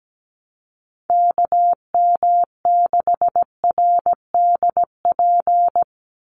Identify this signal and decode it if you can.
Morse code, heard as a keyed tone, about 17 wpm, 700 Hz. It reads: KM6RDP
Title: USA Call Signs 17wpm